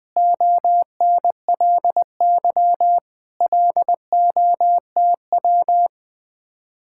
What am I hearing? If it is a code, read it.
Morse code, heard as a keyed tone, about 21 wpm, 700 Hz. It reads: ONLY LOTW
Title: Speed-Racing Common QSO Phrases Encore - 20wpm